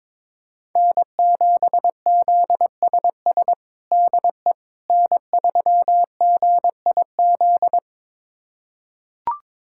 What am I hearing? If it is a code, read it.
Morse code, heard as a keyed tone, about 22 wpm, 700 Hz. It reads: N7ZSS DE N3GIZ
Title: USA Callsign DE Callsign - 5 Characters - 22wpm